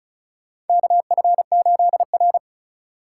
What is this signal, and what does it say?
Morse code, heard as a keyed tone, about 36 wpm, 700 Hz. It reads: KF8R
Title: USA Call Signs - 3x Character Spacing - 35wpm